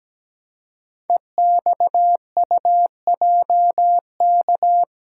Morse code, heard as a keyed tone, about 17 wpm, 700 Hz. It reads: EXUJK
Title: Code Groups - 5 Letters - 17wpm